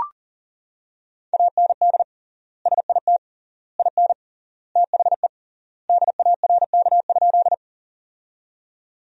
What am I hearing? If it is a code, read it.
Morse code, heard as a keyed tone, about 40 wpm, 700 Hz. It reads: AND SIT IN THE DARK?
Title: 3x Word-Spacing - Sentences from Top 500 Words - 40wpm